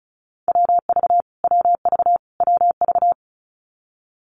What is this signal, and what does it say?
Morse code, heard as a keyed tone, about 34 wpm, 700 Hz. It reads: WV WV WV
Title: US States Repeated 3x 35wpm